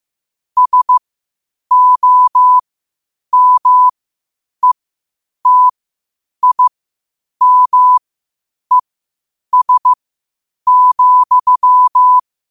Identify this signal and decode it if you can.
Morse code, heard as a keyed tone, about 15 wpm, spaced as for 8 wpm, 1000 Hz. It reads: SOMETIMES,
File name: qotdfiles / 8wpm